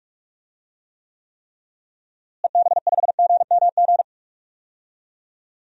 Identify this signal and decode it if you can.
Morse code, heard as a keyed tone, about 46 wpm, 700 Hz. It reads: EB5GMG
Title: DXCC-100 Call Signs - 6 Characters - Repeated 3x - 45wpm